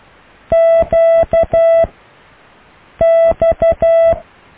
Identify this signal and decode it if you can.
Morse code, heard as a keyed tone, about 12 wpm, 660 Hz. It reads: Q X
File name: cw / Alle letters